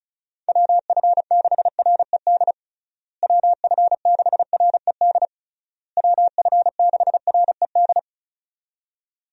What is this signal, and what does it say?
Morse code, heard as a keyed tone, about 36 wpm, 700 Hz. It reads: WF6RED WF6RED WF6RED
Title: USA Call Signs - 6 Characters - Repeated 3x - 35wpm